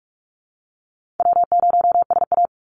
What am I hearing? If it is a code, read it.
Morse code, heard as a keyed tone, about 44 wpm, 700 Hz. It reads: W0SA
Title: Call Signs - 45wpm